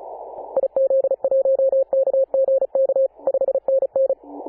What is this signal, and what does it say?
Morse code, heard as a keyed tone, about 34 wpm, 540 Hz. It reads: IZ1KGK 5NN
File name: Apr/29 2042z JD1BLY 14.011MHz CW